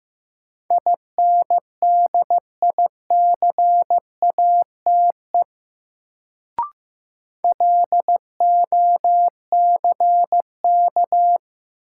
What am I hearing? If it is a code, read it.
Morse code, heard as a keyed tone, about 15 wpm, 700 Hz. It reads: INDICATE LOCK
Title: Sentences from Top 1300 Words - New Word Drill - 15wpm